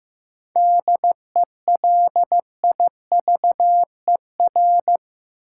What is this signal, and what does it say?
Morse code, heard as a keyed tone, about 15 wpm, 700 Hz. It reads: DELIVER
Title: Seven Letter Words - 15wpm